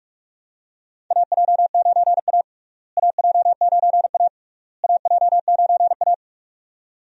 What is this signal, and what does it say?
Morse code, heard as a keyed tone, about 46 wpm, 700 Hz. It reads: AJ9A AJ9A AJ9A
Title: USA Call Signs - 4 Characters - Repeated 3x - 45wpm